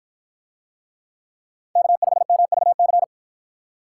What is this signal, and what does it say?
Morse code, heard as a keyed tone, about 50 wpm, 700 Hz. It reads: K5MVC